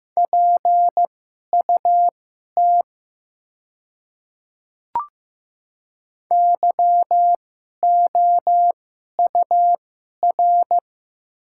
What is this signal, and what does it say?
Morse code, heard as a keyed tone, about 15 wpm, spaced as for 11 wpm, 700 Hz. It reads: PUT YOUR
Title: 2x Character Spacing - Top 200 Words - 15wpm